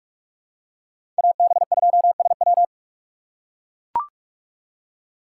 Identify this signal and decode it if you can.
Morse code, heard as a keyed tone, about 41 wpm, 700 Hz. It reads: AB2SW
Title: USA Call Signs - 5 Characters - 4x Character Spacing - 45wpm